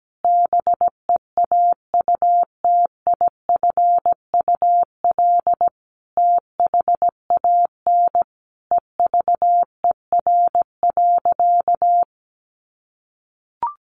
Morse code, heard as a keyed tone, about 17 wpm, 700 Hz. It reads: BEAUTIFUL THAN EVER.
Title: Sentences from Top 1200 Words 17wpm